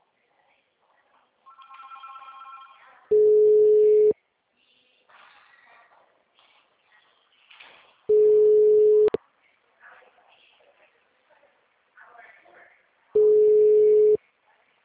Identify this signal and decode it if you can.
Ringback tone (Europe, 425 Hz), with speech.